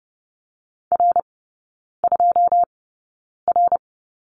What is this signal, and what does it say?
Morse code, heard as a keyed tone, about 30 wpm, spaced as for 9 wpm, 700 Hz. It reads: R2R
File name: 081 Lesson 14e - Number 2 - Three Characters 30wpm